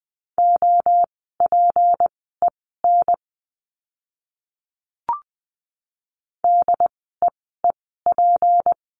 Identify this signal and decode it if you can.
Morse code, heard as a keyed tone, about 20 wpm, spaced as for 14 wpm, 700 Hz. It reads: OPEN DEEP
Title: 153 Lesson 25f - Letter p - Words 20wpm